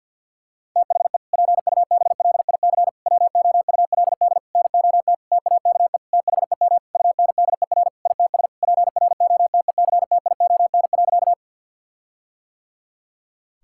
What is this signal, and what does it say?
Morse code, heard as a keyed tone, about 52 wpm, 700 Hz. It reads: THE PUBLIC WOULD NOT TAKE THEM UNDER ITS PROTECTION.
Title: Sentences from Top 1600 Words 50wpm